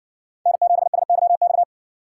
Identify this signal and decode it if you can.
Morse code, heard as a keyed tone, about 60 wpm, 700 Hz. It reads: N7SYX